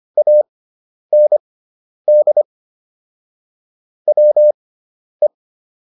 Morse code, heard as a keyed tone, about 25 wpm, spaced as for 10 wpm, 600 Hz. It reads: AND WE